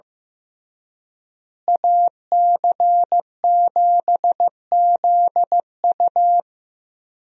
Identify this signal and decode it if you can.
Morse code, heard as a keyed tone, about 15 wpm, 700 Hz. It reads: AC7ZU
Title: USA Call Signs - 5 Characters - Repeated 3x - 15wpm